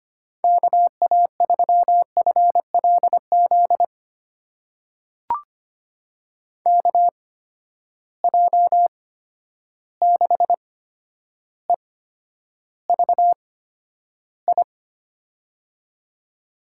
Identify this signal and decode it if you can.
Morse code, heard as a keyed tone, about 25 wpm, spaced as for 11 wpm, 700 Hz. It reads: KA3FLZ KJ6EVI
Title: USA Call Signs - 6 Characters - 8x Character Spacing - 25wpm